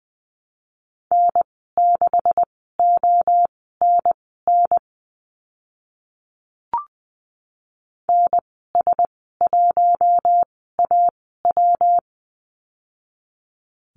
Morse code, heard as a keyed tone, about 20 wpm, spaced as for 14 wpm, 700 Hz. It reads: N6ONN NS1AW
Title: 140 Lesson 23g - Letter f - Call Signs 20wpm